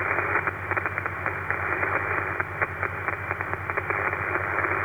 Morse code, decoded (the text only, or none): none